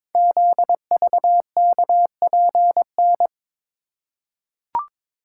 Morse code, heard as a keyed tone, about 21 wpm, 700 Hz. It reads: ZVKPN
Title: Code Groups - 5 Letters - 22wpm